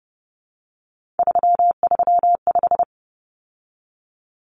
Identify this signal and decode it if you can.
Morse code, heard as a keyed tone, about 30 wpm, 700 Hz. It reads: 335
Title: Numbers - 3-Digit 30wpm